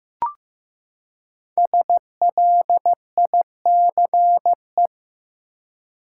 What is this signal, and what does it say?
Morse code, heard as a keyed tone, about 16 wpm, 700 Hz. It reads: SLICE
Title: Five Letter Words - 15wpm